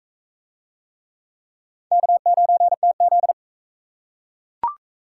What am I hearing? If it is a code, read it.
Morse code, heard as a keyed tone, about 34 wpm, 700 Hz. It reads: K9TZ
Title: Speed Racing - USA 1x2 Call Signs 28wpm